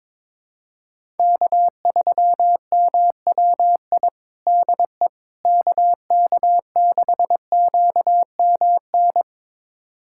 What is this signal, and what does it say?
Morse code, heard as a keyed tone, about 22 wpm, 700 Hz. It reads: K3MWI DE KK6QMN
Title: USA Callsign DE Callsign 22wpm